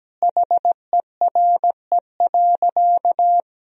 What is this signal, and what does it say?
Morse code, heard as a keyed tone, about 17 wpm, 700 Hz. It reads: HERE.
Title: Sentences from Top 1000 Words - Review Part 4 - 17wpm